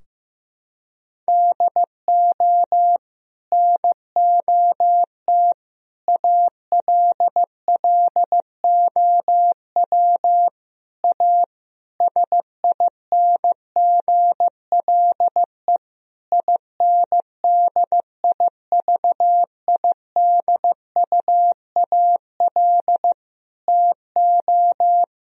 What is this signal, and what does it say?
Morse code, heard as a keyed tone, about 15 wpm, 700 Hz. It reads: DO NOT ALLOW A SINGLE INDIVIDUAL TO
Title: Sentences from Top 1000 Words - Review Part 4 - 15wpm